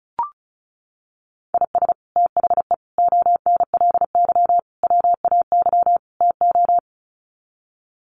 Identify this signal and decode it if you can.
Morse code, heard as a keyed tone, about 34 wpm, 700 Hz. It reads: IS THE ONLY WAY TO
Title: Sets of 5 Words Encore 35wpm